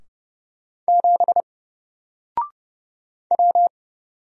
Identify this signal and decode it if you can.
Morse code, heard as a keyed tone, about 30 wpm, 700 Hz. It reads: 7 W
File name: Single-Letter-Number-30wpm